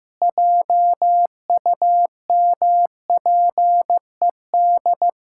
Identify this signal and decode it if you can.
Morse code, heard as a keyed tone, about 15 wpm, 700 Hz. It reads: JUMPED
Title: Pangram Challenge v2 - 15wpm